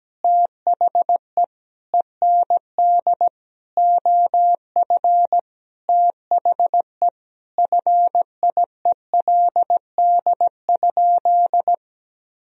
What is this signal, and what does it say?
Morse code, heard as a keyed tone, about 17 wpm, 700 Hz. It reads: THE END OF THE FIELD?
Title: Sentences from Top 1000 Words - Review Part 2 - 17wpm